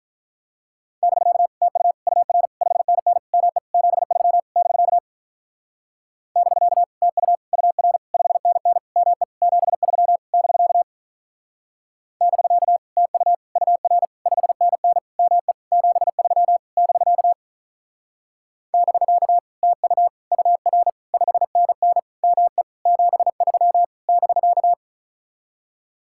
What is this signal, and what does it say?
Morse code, heard as a keyed tone, about 45 wpm, 700 Hz. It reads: <BK> TU UR 5NN ME 73 <BK> <BK> TU UR 5NN ME 73 <BK> <BK> TU UR 5NN ME 73 <BK> <BK> TU UR 5NN ME 73 <BK>